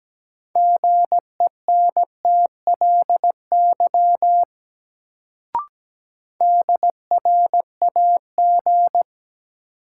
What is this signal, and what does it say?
Morse code, heard as a keyed tone, about 17 wpm, 700 Hz. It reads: GENTLY DRAG
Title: Sentences from Top 1500 Words - New Word Drill - 17wpm